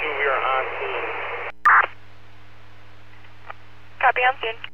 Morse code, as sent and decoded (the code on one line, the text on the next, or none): none